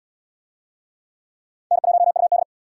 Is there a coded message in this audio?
Yes: Morse code from 1.7 to 2.4 s, heard as a keyed tone, about 72 wpm, 700 Hz. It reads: N1UD